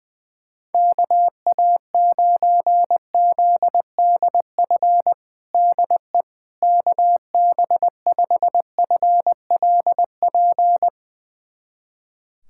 Morse code, heard as a keyed tone, about 20 wpm, 700 Hz. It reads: KA9ZDF DE KB5FLP
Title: USA Callsign DE Callsign - 6 Characters - 20wpm